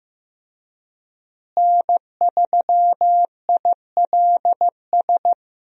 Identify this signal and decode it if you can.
Morse code, heard as a keyed tone, about 15 wpm, 700 Hz. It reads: N3ILS
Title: USA Call Signs - 5 Characters - Repeated 2x - 15wpm